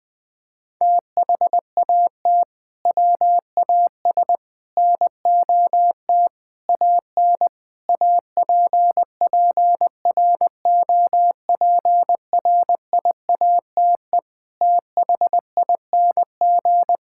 Morse code, heard as a keyed tone, about 20 wpm, 700 Hz. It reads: THAT WAS NOT AN APPROPRIATE THING
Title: Sentences from Top 1000 Words - Review Part 5 - 20wpm